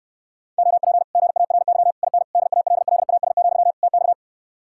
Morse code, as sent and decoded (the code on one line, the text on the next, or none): --- .--. / --. .- .-. -.-- / .. -. / -.. .- .-.. .-.. .- ... --..-- / - -..-
OP GARY IN DALLAS, TX